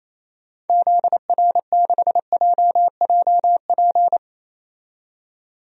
Morse code, heard as a keyed tone, about 28 wpm, 700 Hz. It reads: ZR6JJP
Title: DXCC-100 Call Signs - 6 Characters - 28wpm